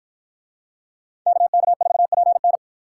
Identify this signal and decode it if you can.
Morse code, heard as a keyed tone, about 53 wpm, 700 Hz. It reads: KK4PN